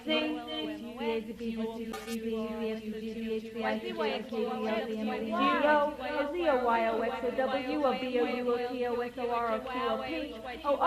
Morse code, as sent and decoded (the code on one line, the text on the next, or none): none